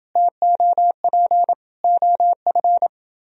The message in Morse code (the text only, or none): TOP OF